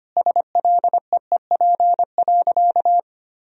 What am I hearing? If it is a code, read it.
Morse code, heard as a keyed tone, about 25 wpm, 700 Hz. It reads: SLEEP.
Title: Sentences from Top 500 Words and Abbreviations 25wpm